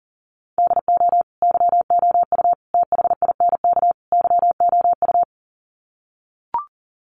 Morse code, heard as a keyed tone, about 37 wpm, 700 Hz. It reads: DO YOU THINK YOU
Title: Sets of 4 Words 40wpm